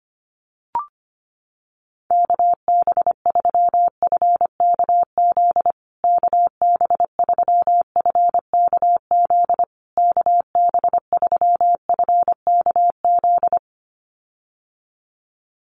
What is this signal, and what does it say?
Morse code, heard as a keyed tone, about 25 wpm, 700 Hz. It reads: KB3FKZ KB3FKZ KB3FKZ
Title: USA Call Signs Repeated 3x 25wpm